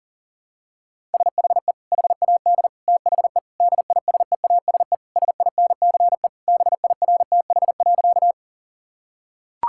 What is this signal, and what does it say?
Morse code, heard as a keyed tone, about 37 wpm, 700 Hz. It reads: SHE HAD THE DISEASE SINCE BIRTH.
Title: Sentences from Top 1000 Words - Review Part 6 - 40wpm